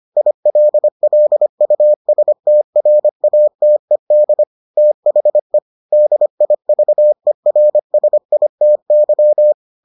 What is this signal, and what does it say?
Morse code, heard as a keyed tone, about 25 wpm, 600 Hz. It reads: ILLUSTRATED THE DIVERSITY